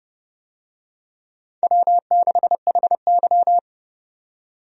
Morse code, heard as a keyed tone, about 30 wpm, 700 Hz. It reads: W6HY
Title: USA Call Signs - 4 Characters - 30wpm